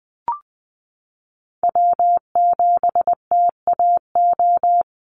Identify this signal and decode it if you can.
Morse code, heard as a keyed tone, about 20 wpm, 700 Hz. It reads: W7TAO
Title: NA SOTA Chaser Callsigns 20wpm